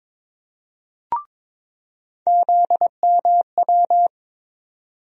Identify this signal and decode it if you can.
Morse code, heard as a keyed tone, about 22 wpm, 700 Hz. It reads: ZMW
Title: USA Call Sign Postfixes - 3 Letters 22wpm